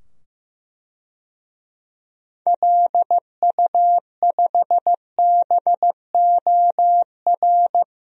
Morse code, heard as a keyed tone, about 16 wpm, 700 Hz. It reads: LU5BOR
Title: DXCC-100 Call Signs - 6 Characters - 15wpm